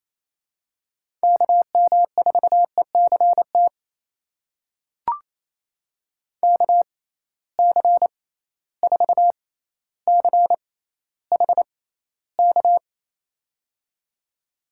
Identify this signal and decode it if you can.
Morse code, heard as a keyed tone, about 28 wpm, spaced as for 14 wpm, 700 Hz. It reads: KM4ECT KC4CHK
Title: USA Call Signs - 6 Characters - 6x Character Spacing - 28wpm